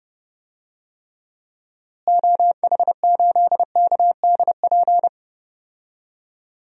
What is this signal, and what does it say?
Morse code, heard as a keyed tone, about 29 wpm, 700 Hz. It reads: OH8KDP
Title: DXCC-100 Call Signs - 6 Characters - Repeated 2x - 30wpm